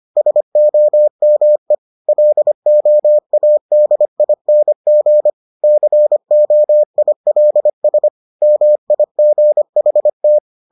Morse code, heard as a keyed tone, about 25 wpm, 600 Hz. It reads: SOME LOADING COILS MIGHT